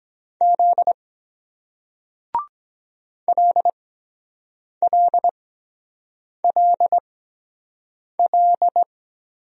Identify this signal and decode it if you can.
Morse code, heard as a keyed tone, about 23 wpm, spaced as for 7 wpm, 700 Hz. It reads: Z LLLL